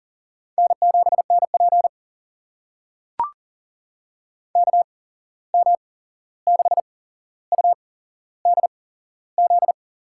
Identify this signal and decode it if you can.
Morse code, heard as a keyed tone, about 40 wpm, spaced as for 15 wpm, 700 Hz. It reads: N7NP KM6UDZ
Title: USA Call Signs - 8x Character Spacing - 40wpm